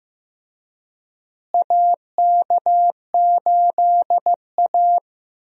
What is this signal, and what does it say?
Morse code, heard as a keyed tone, about 15 wpm, 700 Hz. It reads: AK8A
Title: USA Call Signs - 4 Characters - Repeated 3x - 15wpm